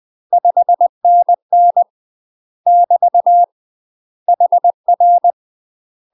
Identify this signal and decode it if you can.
Morse code, heard as a keyed tone, about 20 wpm, 700 Hz. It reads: 5NN <BT> HR
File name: UK MASTER QSOs TRAINING / 20WPM / UK MASTER QSOs 0023